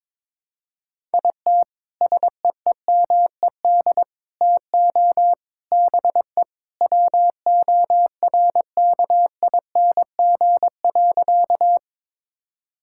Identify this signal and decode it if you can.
Morse code, heard as a keyed tone, about 21 wpm, 700 Hz. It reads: IT SEEMED TO BE WORKING.
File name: practice / 22wpm / Sentences with 3-Word Phrases - 22wpm